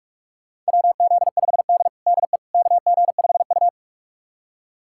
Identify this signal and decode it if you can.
Morse code, heard as a keyed tone, about 44 wpm, 700 Hz. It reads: WZ5D DE KG5U
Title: USA Callsign DE Callsign - 4 Characters - 45wpm